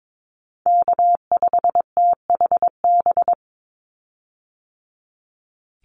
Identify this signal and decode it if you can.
Morse code, heard as a keyed tone, about 22 wpm, 700 Hz. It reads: K5THB
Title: USA Call Signs 22wpm